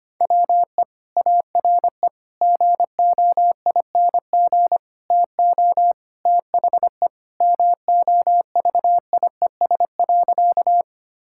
Morse code, heard as a keyed tone, about 25 wpm, 700 Hz. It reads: WE ARE GOING TO THE MOVIES.
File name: practice / 25wpm / Sentences with 4-Word Phrases - 25wpm